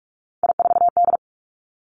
Morse code, heard as a keyed tone, about 46 wpm, 700 Hz. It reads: I4D
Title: Three Challenging Letter-Number Combinations 45wpm